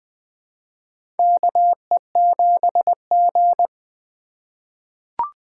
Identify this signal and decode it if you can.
Morse code, heard as a keyed tone, about 21 wpm, 700 Hz. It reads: KE7G
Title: USA Call Signs - 4 Characters - 4x Character Spacing - 20wpm